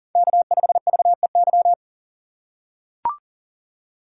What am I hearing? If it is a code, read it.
Morse code, heard as a keyed tone, about 37 wpm, 700 Hz. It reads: K5VEY
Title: USA Call Signs - 5 Characters - 4x Character Spacing - 40wpm